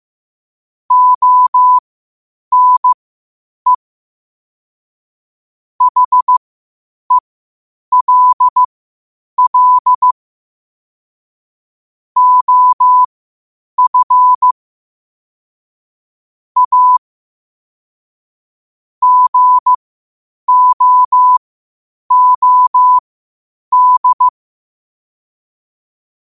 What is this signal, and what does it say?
Morse code, heard as a keyed tone, about 15 wpm, spaced as for 8 wpm, 1000 Hz. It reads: ONE HELL OF A GOOD
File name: qotdfiles / 8wpm